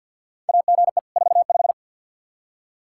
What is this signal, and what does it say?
Morse code, heard as a keyed tone, about 50 wpm, 700 Hz. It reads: AGE 45
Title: Practice QSO - Exchange 3 - More Info - 50wpm